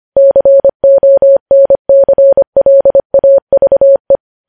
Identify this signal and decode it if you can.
Morse code, heard as a keyed tone, about 25 wpm, 550 Hz. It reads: CONCLAVE
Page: news headlines in morse code at 25 wpm